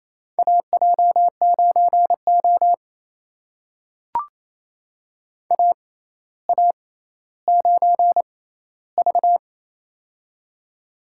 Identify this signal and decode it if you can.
Morse code, heard as a keyed tone, about 28 wpm, spaced as for 14 wpm, 700 Hz. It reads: AJ9O AA9V
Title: USA Call Signs - 4 Characters - 6x Character Spacing - 28wpm